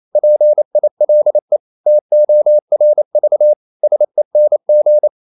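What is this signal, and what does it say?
Morse code, heard as a keyed tone, about 28 wpm, 600 Hz. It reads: PILE TORV SENG
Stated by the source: Danish short words sent in Morse at 28 wpm, from CW med Gnister.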